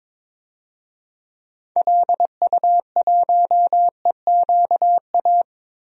Morse code, heard as a keyed tone, about 22 wpm, 700 Hz. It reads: LU1EQA
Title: DXCC-100 Call Signs - 6 Characters - Repeated 3x - 22wpm